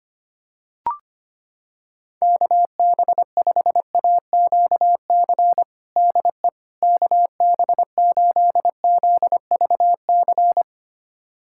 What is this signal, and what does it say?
Morse code, heard as a keyed tone, about 25 wpm, 700 Hz. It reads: KB5AQC DE KB8ZVC
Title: USA Callsign DE Callsign - 6 Characters - 25wpm